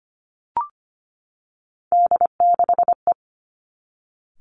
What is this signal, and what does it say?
Morse code, heard as a keyed tone, about 25 wpm, 700 Hz. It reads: D6E
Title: Three Challenging Letter-Number Combinations 25wpm